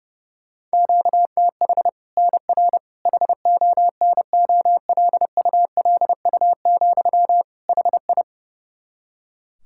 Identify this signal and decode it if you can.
Morse code, heard as a keyed tone, about 30 wpm, 700 Hz. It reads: QTH NR HONOLULU, HI
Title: Practice QSO - Exchange 1 - Beginning - 30wpm